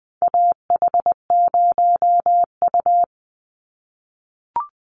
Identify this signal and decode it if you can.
Morse code, heard as a keyed tone, about 20 wpm, 700 Hz. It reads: AH0U
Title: Call Signs - 20wpm